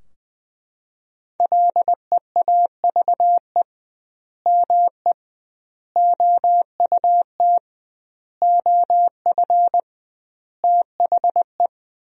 Morse code, heard as a keyed tone, about 20 wpm, 700 Hz. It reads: LEAVE ME OUT OF THE
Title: 2x Word-Spacing - Sentences from Top 400 Words - 20wpm